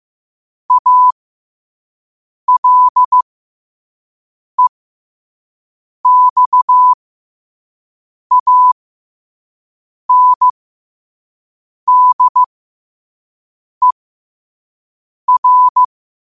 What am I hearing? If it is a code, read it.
Morse code, heard as a keyed tone, about 15 wpm, spaced as for 5 wpm, 1000 Hz. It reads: ALEXANDER
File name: qotdfiles / 5wpm